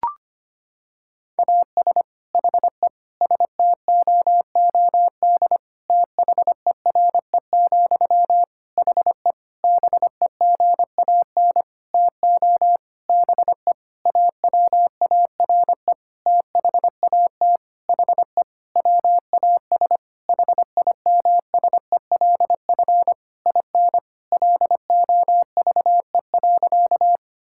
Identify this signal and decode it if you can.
Morse code, heard as a keyed tone, about 25 wpm, 700 Hz. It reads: AS HE STOOD THERE, HE BEGAN TO BE AWARE THAT HE WAS HIMSELF IN LOVE.
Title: Sentences from Top 1600 Words 25wpm